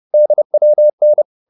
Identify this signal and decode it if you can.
Morse code, heard as a keyed tone, about 30 wpm, 600 Hz. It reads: DWN